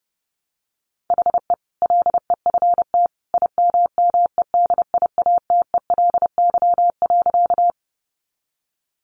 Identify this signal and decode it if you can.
Morse code, heard as a keyed tone, about 30 wpm, 700 Hz. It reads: HE LEFT IMMEDIATELY.
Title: Sentences from Top 1100 Words 30wpm